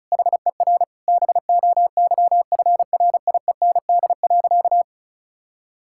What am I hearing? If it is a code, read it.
Morse code, heard as a keyed tone, about 36 wpm, 700 Hz. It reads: HER BOYFRIEND.
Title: Sentences from Top 1000 Words - Review Part 5 - 35wpm